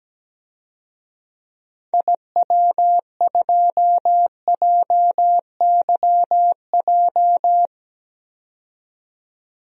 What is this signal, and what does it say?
Morse code, heard as a keyed tone, about 17 wpm, 700 Hz. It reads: IW2JYJ
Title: DXCC-100 Call Signs - 6 Characters - 17wpm